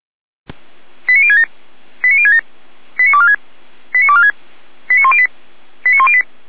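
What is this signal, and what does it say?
Five-tone selective call (ZVEI), heard as quick runs of five tones, tones of about 70 ms. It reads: 89067 89067 89246 89246 89108 89108
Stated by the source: Five-tone ZVEI selective call received by SDR.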